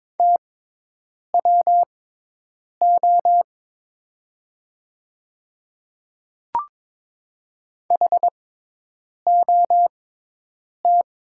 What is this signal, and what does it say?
Morse code, heard as a keyed tone, about 22 wpm, spaced as for 8 wpm, 700 Hz. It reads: TWO HOT
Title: 6x Character Spacing - Top 100 Words - 22wpm